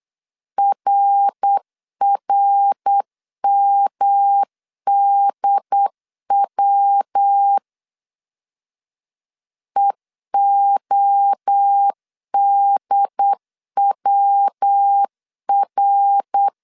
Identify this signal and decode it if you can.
Morse code, heard as a keyed tone, about 8 wpm, 800 Hz. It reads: RRMDW EODWR
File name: CW / DET / 1 3 1500C S B 8wpm 4141